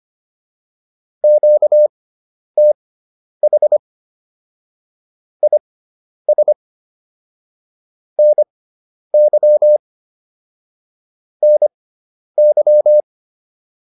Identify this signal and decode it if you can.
Morse code, heard as a keyed tone, about 25 wpm, spaced as for 10 wpm, 600 Hz. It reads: QTH IS NY NY